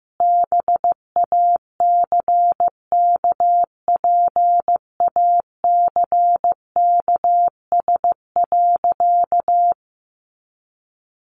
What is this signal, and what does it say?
Morse code, heard as a keyed tone, about 15 wpm, 700 Hz. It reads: BACKPACKS.
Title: Additional Easy Phrases 15wpm